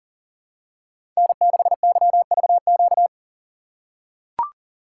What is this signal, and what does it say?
Morse code, heard as a keyed tone, about 40 wpm, 700 Hz. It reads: N6YVQ
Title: USA Call Signs - 5 Characters - 6x Character Spacing - 40wpm